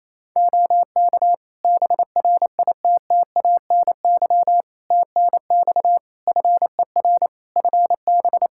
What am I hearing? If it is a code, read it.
Morse code, heard as a keyed tone, about 28 wpm, 700 Hz. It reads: OK BRITTANY TNX FER FB
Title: K1USN SST QSO Practice 28wpm